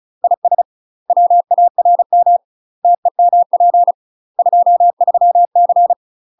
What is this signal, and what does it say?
Morse code, heard as a keyed tone, about 36 wpm, 700 Hz. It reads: IS WARM TEMP 23C
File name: UK MASTER QSOs TRAINING / 35WPM / UK MASTER QSOs 0042